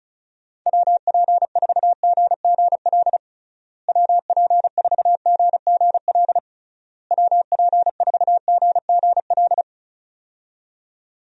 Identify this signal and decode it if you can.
Morse code, heard as a keyed tone, about 36 wpm, 700 Hz. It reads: WP4GGL WP4GGL WP4GGL
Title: USA Call Signs - 6 Characters - Repeated 3x - 35wpm